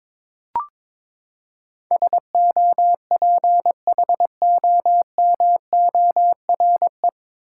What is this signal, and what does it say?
Morse code, heard as a keyed tone, about 22 wpm, 700 Hz. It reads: SOPHOMORE
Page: Morse Code Ninja practice 22wpm